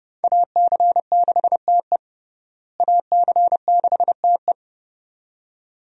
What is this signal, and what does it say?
Morse code, heard as a keyed tone, about 30 wpm, 700 Hz. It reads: AC6TE AC6TE
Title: USA Call Signs - 5 Characters - Repeated 2x - 30wpm